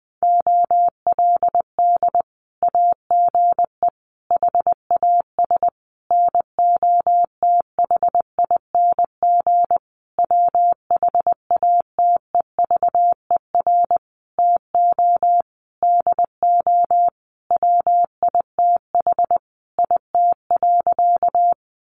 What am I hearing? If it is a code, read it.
Morse code, heard as a keyed tone, about 20 wpm, 700 Hz. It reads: OLD AGE HAS NOTHING WHATEVER TO DO WITH IT.
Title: Sentences from Top 1100 Words 20wpm